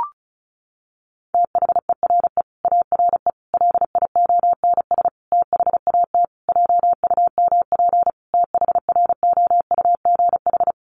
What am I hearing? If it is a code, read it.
Morse code, heard as a keyed tone, about 34 wpm, 700 Hz. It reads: THERE ARE LIONS THAT JUMP THROUGH
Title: Easy Phrases 35wpm